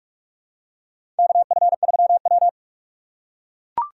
Morse code, heard as a keyed tone, about 43 wpm, 700 Hz. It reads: KF3W
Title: USA Call Signs - 4 Characters - 3x Character Spacing - 45wpm